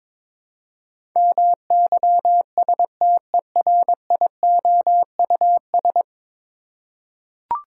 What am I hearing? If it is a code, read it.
Morse code, heard as a keyed tone, about 22 wpm, 700 Hz. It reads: MYSTERIOUS
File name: practice / 22wpm / 10 Letter Words - First 3 Letters Given - 22wpm